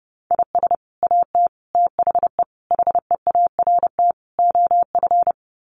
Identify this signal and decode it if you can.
Morse code, heard as a keyed tone, about 30 wpm, 700 Hz. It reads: IS AT THE HEART OF
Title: Sets of 5 Words Encore2 30wpm